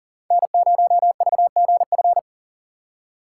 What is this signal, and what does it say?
Morse code, heard as a keyed tone, about 37 wpm, 700 Hz. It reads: N0VGF
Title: USA Call Signs - 5 Characters - 6x Character Spacing - 40wpm